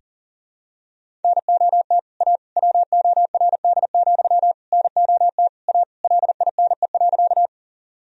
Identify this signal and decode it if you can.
Morse code, heard as a keyed tone, about 40 wpm, 700 Hz. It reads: NOT A WORD, NOT A LINE.
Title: Sentences from Top 1000 Words - Review Part 1 - 40wpm